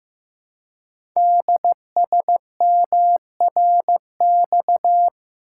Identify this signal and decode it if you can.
Morse code, heard as a keyed tone, about 15 wpm, 700 Hz. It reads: DSMRX
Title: Code Groups - 5 Letters - 15wpm